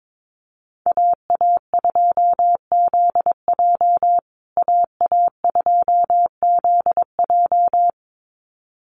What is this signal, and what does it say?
Morse code, heard as a keyed tone, about 22 wpm, 700 Hz. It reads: AA2ZJ AA2ZJ
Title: USA Call Signs Repeated 2x 22wpm